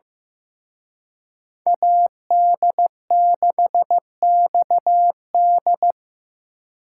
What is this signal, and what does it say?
Morse code, heard as a keyed tone, about 18 wpm, 700 Hz. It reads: AD6XD
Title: Speed Racing - USA 2x2 Call Signs 15wpm